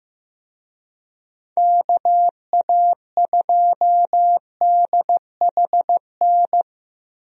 Morse code, heard as a keyed tone, about 16 wpm, 700 Hz. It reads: KA2DHN
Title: USA Call Signs - 6 Characters - Repeated 2x - 15wpm